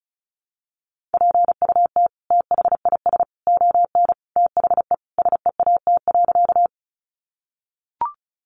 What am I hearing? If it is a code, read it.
Morse code, heard as a keyed tone, about 34 wpm, 700 Hz. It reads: PUT THIS ON THE SEAT.
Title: Sentences from Top 1500 Words 35wpm